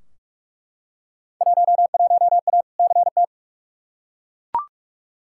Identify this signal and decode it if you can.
Morse code, heard as a keyed tone, about 41 wpm, 700 Hz. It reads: 11A CT
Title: Field Day Exchanges Encore - 45wpm